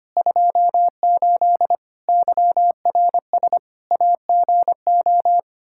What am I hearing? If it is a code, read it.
Morse code, heard as a keyed tone, about 25 wpm, 700 Hz. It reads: 28 YRS AGO
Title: Practice QSO - Exchange 3 - More Info - 25wpm